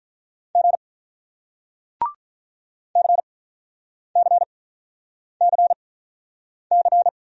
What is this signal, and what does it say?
Morse code, heard as a keyed tone, about 42 wpm, spaced as for 9 wpm, 700 Hz. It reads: G CCCC
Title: Speed-Racing Single Letters 35wpm